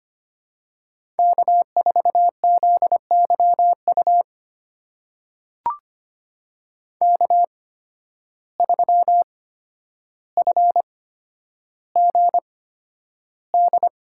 Morse code, heard as a keyed tone, about 25 wpm, spaced as for 11 wpm, 700 Hz. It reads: K4ZYU K3FGD
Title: USA Call Signs - 5 Characters - 8x Character Spacing - 25wpm